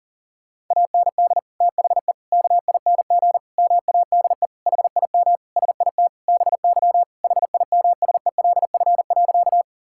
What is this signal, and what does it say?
Morse code, heard as a keyed tone, about 40 wpm, 700 Hz. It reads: AND THE KING MADE HIM SIT BY HIMSELF.
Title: Sentences from Top 1600 Words 40wpm